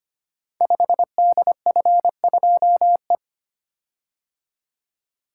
Morse code, heard as a keyed tone, about 25 wpm, 700 Hz. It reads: 5DF2E
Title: Code Groups - 5 Letter:Numbers - 25wpm